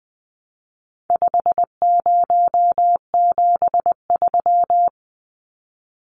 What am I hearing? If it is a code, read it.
Morse code, heard as a keyed tone, about 20 wpm, 700 Hz. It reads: 5073
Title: Numbers - 4-Digit 20wpm